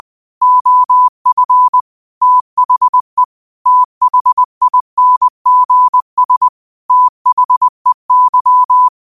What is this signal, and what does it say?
Morse code, heard as a keyed tone, about 20 wpm, 1000 Hz. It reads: OF THE THINGS THEY